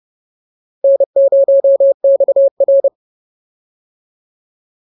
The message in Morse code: N0XR

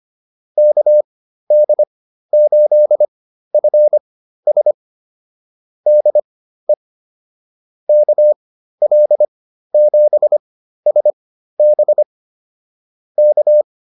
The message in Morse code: KD8FS DE KL7SB K

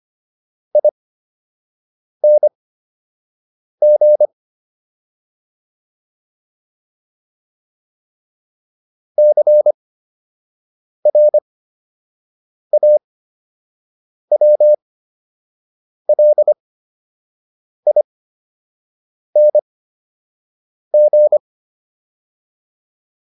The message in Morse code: ING CRAWLING

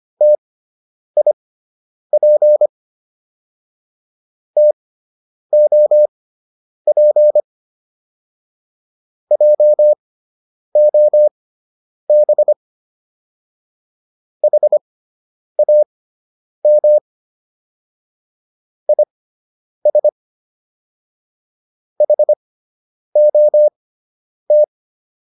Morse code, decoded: TIP TOP JOB HAM IS HOT

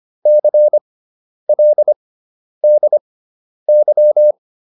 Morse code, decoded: CLDY

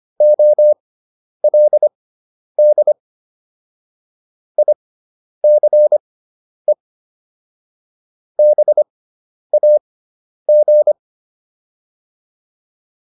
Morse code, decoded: OLD ICE BAG

